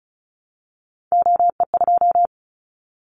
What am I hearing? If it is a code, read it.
Morse code, heard as a keyed tone, about 34 wpm, 700 Hz. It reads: OE2
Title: Callsign Prefixes for Top 25 DXCC - Repeated 3x - 35wpm